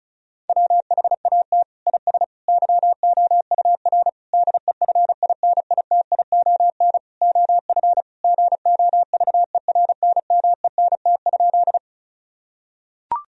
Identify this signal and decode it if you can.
Morse code, heard as a keyed tone, about 36 wpm, 700 Hz. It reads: WHAT IS YOUR DEFINITION OF GOVERNMENT?
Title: Sentences from Top 1000 Words - Review Part 4 - 35wpm